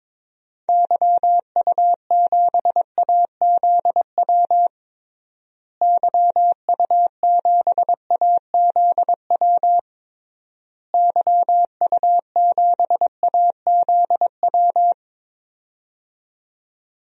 Morse code, heard as a keyed tone, about 21 wpm, 700 Hz. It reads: YU7AZW YU7AZW YU7AZW
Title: DXCC-100 Call Signs - 6 Characters - Repeated 3x - 22wpm